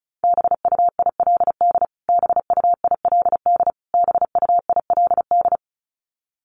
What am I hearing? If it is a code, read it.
Morse code, heard as a keyed tone, about 34 wpm, 700 Hz. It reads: BUILD BUILD BUILD
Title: Top 200 Words Repeated 35wpm